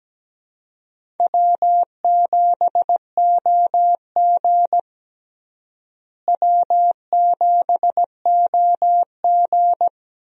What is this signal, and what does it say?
Morse code, heard as a keyed tone, about 17 wpm, 700 Hz. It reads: W7OG W7OG
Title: USA Call Signs - 4 Characters - Repeated 3x - 17wpm